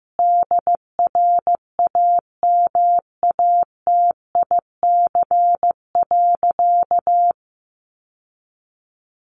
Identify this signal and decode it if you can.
Morse code, heard as a keyed tone, about 14 wpm, 700 Hz. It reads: DRAMATIC.
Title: Sentences from Top 1300 Words 15wpm